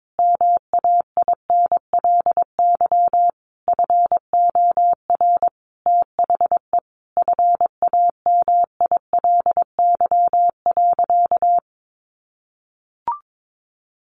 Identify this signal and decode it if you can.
Morse code, heard as a keyed tone, about 22 wpm, 700 Hz. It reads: MAINLY FOR THE FAMILY.
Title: Sentences from Top 1400 Words 22wpm